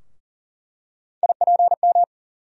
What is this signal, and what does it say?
Morse code, heard as a keyed tone, about 40 wpm, 700 Hz. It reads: IPM
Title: USA Call Sign Postfixes - 3 Letters 40wpm